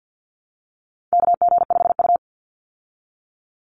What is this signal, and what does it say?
Morse code, heard as a keyed tone, about 50 wpm, 700 Hz. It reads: KG5U
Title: Call Signs Repeated 3x - 50wpm